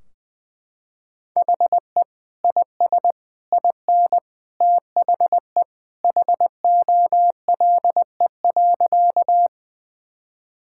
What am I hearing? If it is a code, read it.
Morse code, heard as a keyed tone, about 19 wpm, 700 Hz. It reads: HE IS IN THE HOLE.
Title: Sentences from Top 1600 Words 20wpm